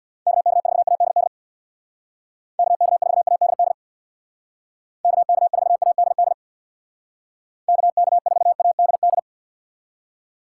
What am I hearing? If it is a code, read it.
Morse code, heard as a keyed tone, about 60 wpm, 700 Hz. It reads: KK4ADD KK4ADD KK4ADD KK4ADD
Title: Speed Racing - USA 2x3 Call Signs 50wpm